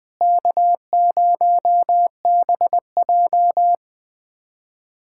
Morse code, heard as a keyed tone, about 21 wpm, 700 Hz. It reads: K0BJ
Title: USA Callsign DE Callsign - 4 Characters - 20wpm